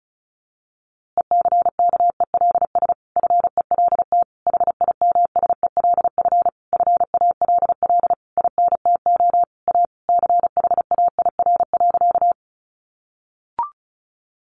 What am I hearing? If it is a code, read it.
Morse code, heard as a keyed tone, about 34 wpm, 700 Hz. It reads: ECKELS FELT HIMSELF FALL INTO A CHAIR.
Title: A Sound Of Thunder 35wpm